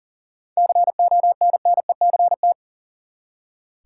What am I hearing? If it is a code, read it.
Morse code, heard as a keyed tone, about 40 wpm, 700 Hz. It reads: CONNECT
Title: Seven Letter Words - 40wpm